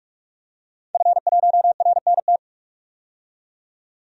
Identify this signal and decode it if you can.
Morse code, heard as a keyed tone, about 42 wpm, 700 Hz. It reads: F1RNT
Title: DXCC-100 Call Signs - 5 Characters - 45wpm